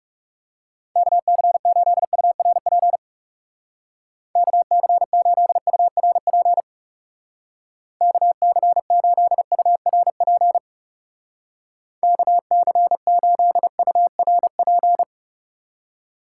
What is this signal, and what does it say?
Morse code, heard as a keyed tone, about 37 wpm, 700 Hz. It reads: KC8URP KC8URP KC8URP KC8URP
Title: Speed Racing - USA 2x3 Call Signs 30wpm